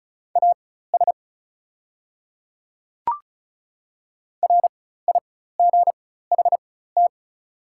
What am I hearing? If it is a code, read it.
Morse code, heard as a keyed tone, about 34 wpm, spaced as for 16 wpm, 700 Hz. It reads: AS RIGHT
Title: 4x Character Spacing - Top 200 Words - 35wpm